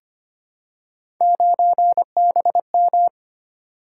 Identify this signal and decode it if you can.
Morse code, heard as a keyed tone, about 25 wpm, 700 Hz. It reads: 9BM
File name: Three Letter-Number Combinations - 25wpm